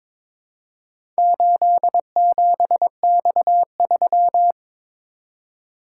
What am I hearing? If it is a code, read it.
Morse code, heard as a keyed tone, about 22 wpm, 700 Hz. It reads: 87X3
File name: practice / 22wpm / Four Letter-Number Combinations - 22wpm